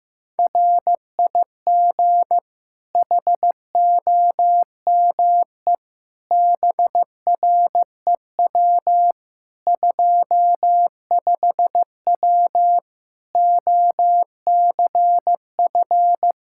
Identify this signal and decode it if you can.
Morse code, heard as a keyed tone, about 15 wpm, 700 Hz. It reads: RIG HOME BREW 25W OCF
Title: Practice QSO - Exchange 2 - Info - 15wpm